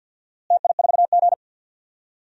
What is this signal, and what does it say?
Morse code, heard as a keyed tone, about 48 wpm, 700 Hz. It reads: TI4G
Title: DXCC-100 Call Signs - 4 Characters - Repeated 3x - 50wpm